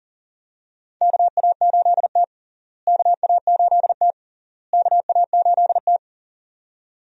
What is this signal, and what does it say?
Morse code, heard as a keyed tone, about 40 wpm, 700 Hz. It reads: KA8T KA8T KA8T
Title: USA Call Signs - 4 Characters - Repeated 3x - 40wpm